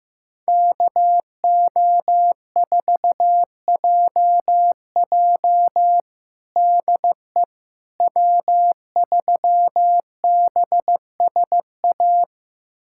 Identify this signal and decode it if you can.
Morse code, heard as a keyed tone, about 15 wpm, 700 Hz. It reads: KO4JJ DE W3BSA
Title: USA Callsign DE Callsign - 5 Characters - 15wpm